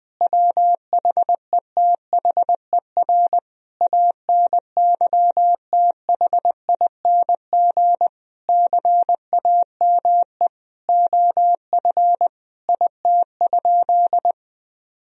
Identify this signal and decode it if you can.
Morse code, heard as a keyed tone, about 20 wpm, 700 Hz. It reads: WHETHER ANYTHING CAME OF IT?
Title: Sentences from Top 1000 Words - Review Part 3 - 20wpm